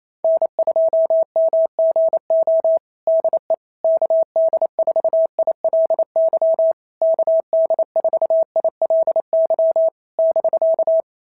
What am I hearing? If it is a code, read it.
Morse code, heard as a keyed tone, about 29 wpm, 690 Hz. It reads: N2MGO DE KD4ILY KD4ILY <BK>
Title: SOTA QSO Practice 28wpm